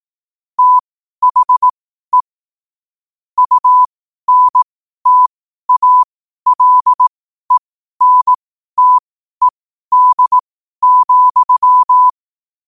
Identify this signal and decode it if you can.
Morse code, heard as a keyed tone, about 18 wpm, spaced as for 12 wpm, 1000 Hz. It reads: THE UNTALENTED,
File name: qotdfiles / 12wpm